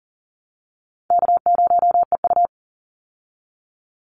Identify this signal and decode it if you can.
Morse code, heard as a keyed tone, about 40 wpm, 700 Hz. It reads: K0EU
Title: Call Signs Repeated 3x - 40wpm